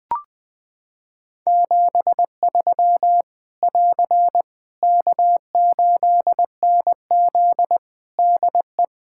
Morse code, heard as a keyed tone, about 20 wpm, 700 Hz. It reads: 73 <AR> K8NZ DE
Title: Practice QSO - Exchange 4 - Ending - 20wpm